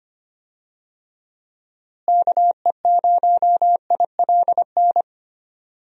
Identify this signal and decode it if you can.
Morse code, heard as a keyed tone, about 25 wpm, 700 Hz. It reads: KE0ILN
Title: USA Call Signs - 6 Characters - Repeated 2x - 25wpm